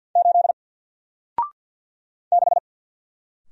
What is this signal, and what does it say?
Morse code, heard as a keyed tone, about 48 wpm, 700 Hz. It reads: 8 6
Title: Single Letter-Number Alt 50wpm